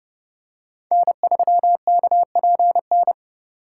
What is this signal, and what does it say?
Morse code, heard as a keyed tone, about 30 wpm, 700 Hz. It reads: N3KPN
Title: USA Call Signs - 2x Character Spacing - 30wpm